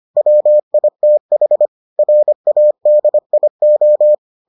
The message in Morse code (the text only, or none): WITH RADIO